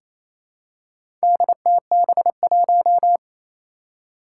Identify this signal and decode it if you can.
Morse code, heard as a keyed tone, about 28 wpm, 700 Hz. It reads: DTB1
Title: Four Challenging Letter-Number Combinations 28wpm